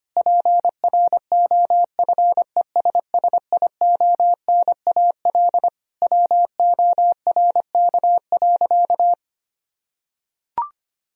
Morse code, heard as a keyed tone, about 25 wpm, 700 Hz. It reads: PROFESSIONAL WORK.
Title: Sentences from Top 1000 Words - Review Part 4 - 25wpm